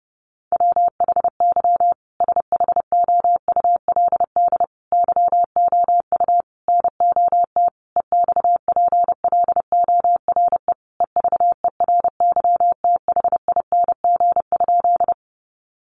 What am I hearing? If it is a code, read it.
Morse code, heard as a keyed tone, about 30 wpm, 700 Hz. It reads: WHY SHOULD YOU NOT EXPLORE EVERYTHING?
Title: Sentences from Top 1100 Words 30wpm